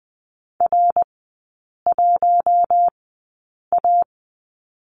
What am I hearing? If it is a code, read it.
Morse code, heard as a keyed tone, about 20 wpm, spaced as for 8 wpm, 700 Hz. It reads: R1A
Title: 060 Lesson 11e - Letter h - Three Characters 20wpm